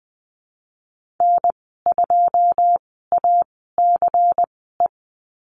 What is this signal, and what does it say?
Morse code, heard as a keyed tone, about 20 wpm, spaced as for 14 wpm, 700 Hz. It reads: N2ACE
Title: 186 Lesson 30g - Letter b - Call Signs 20wpm